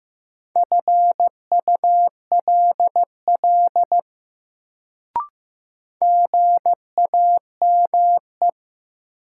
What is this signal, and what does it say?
Morse code, heard as a keyed tone, about 15 wpm, 700 Hz. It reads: FULL GAME
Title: Sentences from Top 500 Words - New Word Drill - 15wpm